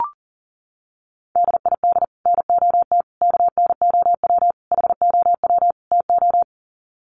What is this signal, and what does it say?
Morse code, heard as a keyed tone, about 40 wpm, 700 Hz. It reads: DID NOT KNOW HOW TO
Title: Sets of 5 Words 40wpm